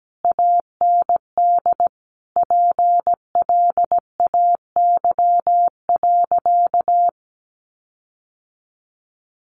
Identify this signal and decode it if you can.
Morse code, heard as a keyed tone, about 17 wpm, 700 Hz. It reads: AND PLAY.
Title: Sentences from Top 1300 Words 17wpm